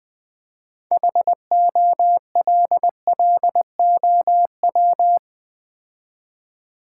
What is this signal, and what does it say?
Morse code, heard as a keyed tone, about 21 wpm, 690 Hz. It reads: HOLLOW
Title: Build A Word - 3 Letters At A Time - 6-Letter Words - 20wpm